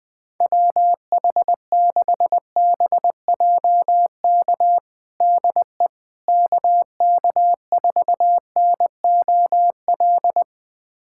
Morse code, heard as a keyed tone, about 20 wpm, 700 Hz. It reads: WH6BJK DE KK4NOL
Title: USA Callsign DE Callsign 20wpm